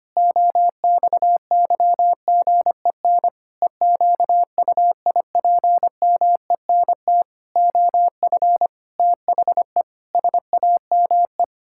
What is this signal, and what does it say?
Morse code, heard as a keyed tone, about 25 wpm, 700 Hz. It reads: OXYGEN EQUIPMENT OF THE SAME
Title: Pangram Challenge v2 - 25wpm